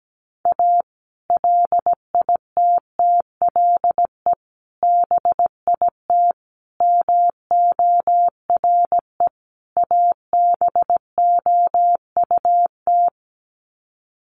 Morse code, heard as a keyed tone, about 17 wpm, 700 Hz. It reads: A LITTLE BIT MORE ABOUT
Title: Sets of 5 Words Encore2 17wpm